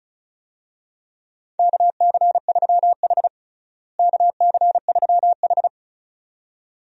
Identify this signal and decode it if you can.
Morse code, heard as a keyed tone, about 34 wpm, 700 Hz. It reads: KC3H KC3H
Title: USA Call Signs - 4 Characters - Repeated 2x - 35wpm